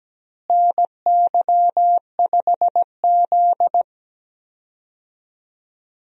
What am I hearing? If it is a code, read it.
Morse code, heard as a keyed tone, about 17 wpm, 700 Hz. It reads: NY5Z
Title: USA Call Signs - 4 Characters - Repeated 3x - 17wpm